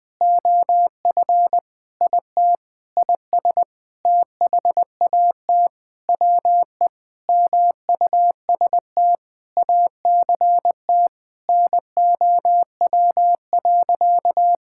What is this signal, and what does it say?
Morse code, heard as a keyed tone, about 20 wpm, 700 Hz. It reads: OF IT IS THAT WE MUST ACT NOW.
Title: Sentences with 9-Word Phrases - 20wpm